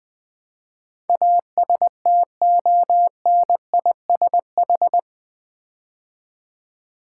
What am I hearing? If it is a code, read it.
Morse code, heard as a keyed tone, about 20 wpm, 700 Hz. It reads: ASTONISH
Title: Build A Word - 4 Letters At A Time - 11-Letter Words - 20wpm